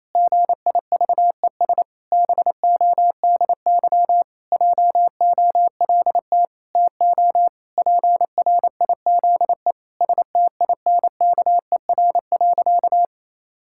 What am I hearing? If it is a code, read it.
Morse code, heard as a keyed tone, about 28 wpm, 700 Hz. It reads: GIVES BODY JOLT TO PRIZE STINKER.
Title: Pangram Challenge v2 - 28wpm